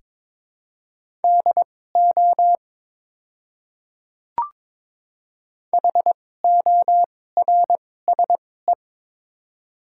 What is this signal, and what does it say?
Morse code, heard as a keyed tone, about 22 wpm, spaced as for 16 wpm, 700 Hz. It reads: DO HORSE
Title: 2x Character Spacing - Top 300 Words - 22wpm